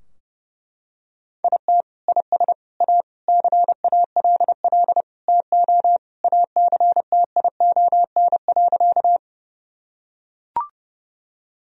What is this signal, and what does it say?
Morse code, heard as a keyed tone, about 30 wpm, 700 Hz. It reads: IT IS A CALL TO ACTION.
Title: Sentences from Top 1000 Words - Review Part 4 - 30wpm